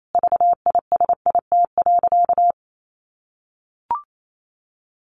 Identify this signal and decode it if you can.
Morse code, heard as a keyed tone, about 28 wpm, 700 Hz. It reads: VISIT.
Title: Sentences from Top 1200 Words 28wpm